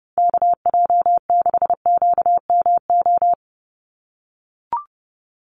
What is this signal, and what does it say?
Morse code, heard as a keyed tone, about 30 wpm, 700 Hz. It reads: KJ6QMO
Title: USA Call Signs 30wpm